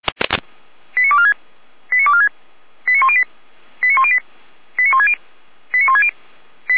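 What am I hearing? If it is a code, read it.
Five-tone selective call (ZVEI), heard as quick runs of five tones, tones of about 70 ms. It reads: 89246 89246 89108 89108 89160 89160
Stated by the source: ZVEI five-tone call received on SDR.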